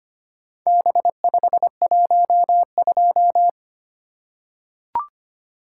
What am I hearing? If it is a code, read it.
Morse code, heard as a keyed tone, about 25 wpm, 700 Hz. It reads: B512
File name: practice / 25wpm / Four Letter-Number Combinations - 25wpm